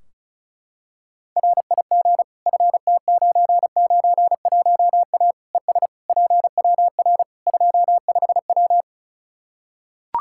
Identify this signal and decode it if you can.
Morse code, heard as a keyed tone, about 36 wpm, 700 Hz. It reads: RIG FT991A ES PWR 25W
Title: Practice QSO - Exchange 2 - Info - 35wpm